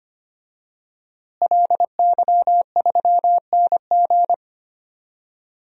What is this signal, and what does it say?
Morse code, heard as a keyed tone, about 25 wpm, 700 Hz. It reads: LY3NG
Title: DXCC-100 Call Signs - 5 Characters - Repeated 2x - 25wpm